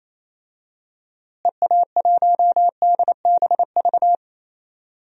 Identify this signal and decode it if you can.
Morse code, heard as a keyed tone, about 28 wpm, 700 Hz. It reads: EA1DBV
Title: DXCC-100 Call Signs - 6 Characters - Repeated 3x - 28wpm